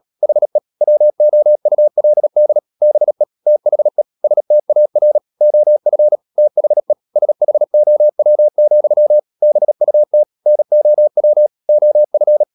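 Morse code, heard as a keyed tone, about 37 wpm, 600 Hz. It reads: HE WOULD BE THE STAR OF THE SHOW, BUT NOW OF
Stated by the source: Morse code practice at 35–39 wpm sending English text of Ken Follett's novel Never, from CW med Gnister.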